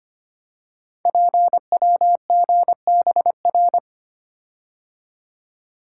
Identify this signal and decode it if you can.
Morse code, heard as a keyed tone, about 25 wpm, 700 Hz. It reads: PWGBR
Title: Code Groups - 5 Letters - 25wpm